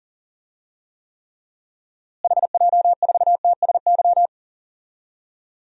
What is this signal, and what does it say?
Morse code, heard as a keyed tone, about 37 wpm, 700 Hz. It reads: HJ4TSY
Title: DXCC-100 Call Signs - 6 Characters - Repeated 3x - 40wpm